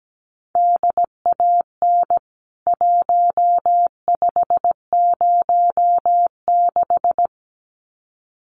Morse code, heard as a keyed tone, about 17 wpm, 700 Hz. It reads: DAN 1506
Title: CWT Exchange 17wpm